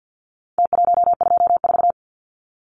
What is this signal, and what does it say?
Morse code, heard as a keyed tone, about 50 wpm, 700 Hz. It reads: T124
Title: Numbers - Contest Style - 4-Digit 50wpm